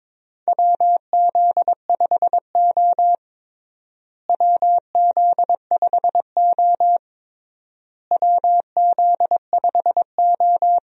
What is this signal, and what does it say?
Morse code, heard as a keyed tone, about 22 wpm, 700 Hz. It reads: WZ5O WZ5O WZ5O
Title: USA Call Signs - 4 Characters - Repeated 3x - 22wpm